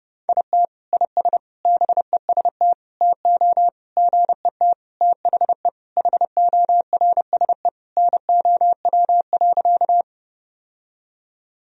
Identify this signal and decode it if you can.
Morse code, heard as a keyed tone, about 32 wpm, 700 Hz. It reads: IT IS BEST TO GET THE HORSE NOW.
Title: Sentences from Top 1000 Words - Review Part 1 - 30wpm